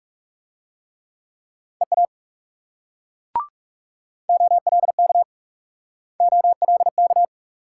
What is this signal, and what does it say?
Morse code, heard as a keyed tone, about 38 wpm, 700 Hz. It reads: EA OLK OLK